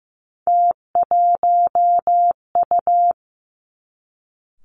Morse code, heard as a keyed tone, about 15 wpm, 700 Hz. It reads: T1U
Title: Three Challenging Letter-Number Combinations 15wpm